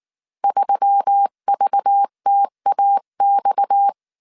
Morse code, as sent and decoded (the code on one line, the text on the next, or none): ...-- ...- - .- -..-
3VTAX